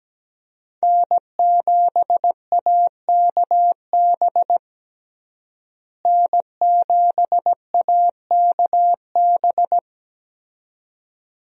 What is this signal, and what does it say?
Morse code, heard as a keyed tone, about 17 wpm, 700 Hz. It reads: N7AKB N7AKB
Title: USA Call Signs - 5 Characters - Repeated 2x - 17wpm